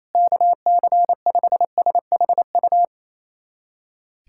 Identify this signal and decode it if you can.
Morse code, heard as a keyed tone, about 28 wpm, 700 Hz. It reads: KC5SHU
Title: USA Callsign DE Callsign - 6 Characters - 28wpm